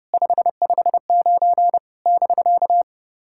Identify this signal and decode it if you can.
Morse code, heard as a keyed tone, about 30 wpm, 700 Hz. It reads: 559 <BK>
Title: SOTA QSO Practice 30wpm